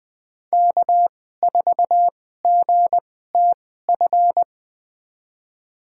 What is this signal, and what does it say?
Morse code, heard as a keyed tone, about 20 wpm, spaced as for 14 wpm, 700 Hz. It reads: K4GTF
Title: USA Call Signs - 5 Characters - 2x Character Spacing - 20wpm